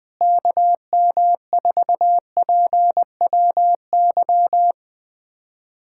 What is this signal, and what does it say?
Morse code, heard as a keyed tone, about 20 wpm, 700 Hz. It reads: KM4PWY
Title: USA Callsign DE Callsign 20wpm